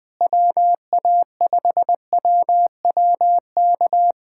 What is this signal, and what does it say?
Morse code, heard as a keyed tone, about 20 wpm, 700 Hz. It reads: WA5WWK
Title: USA Callsign DE Callsign - 6 Characters - 20wpm